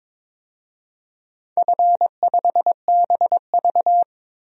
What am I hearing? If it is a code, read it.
Morse code, heard as a keyed tone, about 22 wpm, 700 Hz. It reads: F5BV
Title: DXCC-100 Call Signs - 4 Characters - Repeated 2x - 22wpm